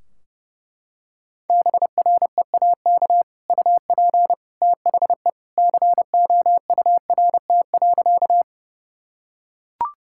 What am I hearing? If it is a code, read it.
Morse code, heard as a keyed tone, about 32 wpm, 700 Hz. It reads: BREAK UP THE COURT.
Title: Sentences from Top 1600 Words 30wpm